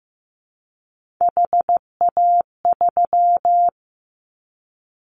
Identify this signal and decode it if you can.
Morse code, heard as a keyed tone, about 15 wpm, 700 Hz. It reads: HA3
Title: Callsign Prefixes for Top 25 DXCC - Repeated 3x - 15wpm